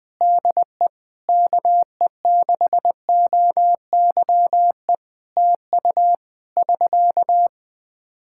Morse code, heard as a keyed tone, about 20 wpm, 700 Hz. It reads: DE KE6OYE TU <SK>
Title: Practice QSO - Exchange 4 - Ending - 20wpm